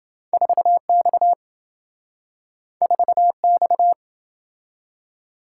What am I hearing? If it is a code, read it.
Morse code, heard as a keyed tone, about 27 wpm, 700 Hz. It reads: 4X 4X